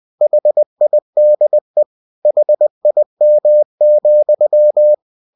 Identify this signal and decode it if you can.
Morse code, heard as a keyed tone, about 20 wpm, 600 Hz. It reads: HIDE HIM,
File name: cw-books / winnie-the-pooh / winnie-20wpm-page-0061